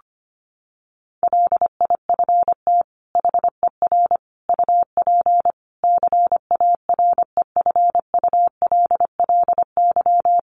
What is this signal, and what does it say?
Morse code, heard as a keyed tone, about 25 wpm, 700 Hz. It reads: LIFT HER UP CAREFULLY
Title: Sentences from Top 1300 Words 25wpm